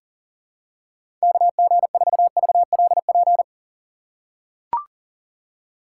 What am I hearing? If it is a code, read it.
Morse code, heard as a keyed tone, about 40 wpm, 700 Hz. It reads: KG4VLP
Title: USA Call Signs - 6 Characters - 2x Character Spacing - 40wpm